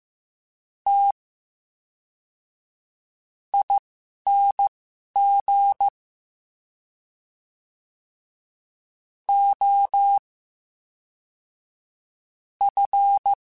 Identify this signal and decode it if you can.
Morse code, heard as a keyed tone, about 15 wpm, spaced as for 5 wpm, 800 Hz. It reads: TING OF
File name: Special INGs / Special INGs-mp3s / VerySlowToFaster Ing @10